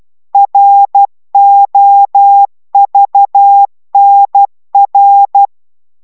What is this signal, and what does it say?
Morse code, heard as a keyed tone, about 12 wpm, 800 Hz. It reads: ROVNR